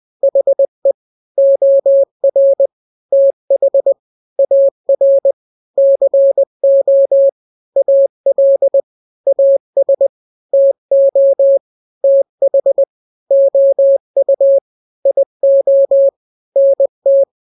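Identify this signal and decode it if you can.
Morse code, heard as a keyed tone, about 20 wpm, 550 Hz. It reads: HE OR TH AR CO AL AS TO TH OU IO NT
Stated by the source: Morse drill of two-letter n-grams at 20/18 WPM (Farnsworth), from KK7RR.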